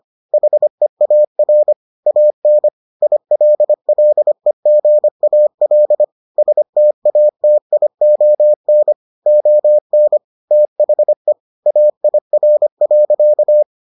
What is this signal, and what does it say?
Morse code, heard as a keyed tone, about 25 wpm, 600 Hz. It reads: HEAR AN ILLEGAL STATION ON THE AIR.